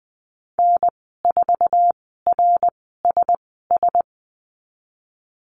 Morse code, heard as a keyed tone, about 20 wpm, spaced as for 15 wpm, 700 Hz. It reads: N4RSS
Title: 207 Lesson 33g - Letter j - Call Signs 20wpm